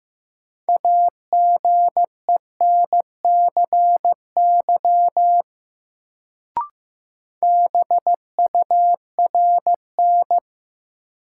Morse code, heard as a keyed tone, about 15 wpm, 700 Hz. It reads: AGENCY BURN
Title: Sentences from Top 1300 Words - New Word Drill Alt - 15wpm